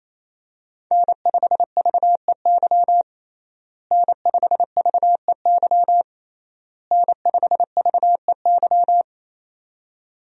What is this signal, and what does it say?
Morse code, heard as a keyed tone, about 27 wpm, 700 Hz. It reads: N5VEY N5VEY N5VEY
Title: USA Call Signs - 5 Characters - Repeated 3x - 28wpm